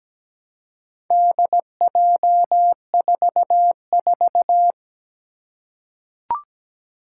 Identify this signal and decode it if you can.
Morse code, heard as a keyed tone, about 17 wpm, 700 Hz. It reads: DJ44
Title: Four Challenging Letter-Number Combinations 17wpm